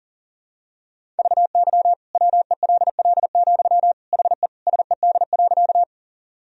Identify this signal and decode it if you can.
Morse code, heard as a keyed tone, about 40 wpm, 700 Hz. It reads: VY WELL, HE SED.
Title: Sentences from Top 200 Words and Abbreviations 40wpm